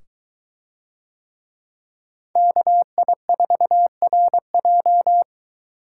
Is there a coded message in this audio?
Yes: Morse code from 2.4 to 5.2 s, heard as a keyed tone, about 23 wpm, 700 Hz. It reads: KI4RJ